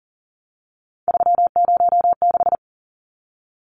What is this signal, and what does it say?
Morse code, heard as a keyed tone, about 40 wpm, 700 Hz. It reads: 306
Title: Numbers - 3-Digit 40wpm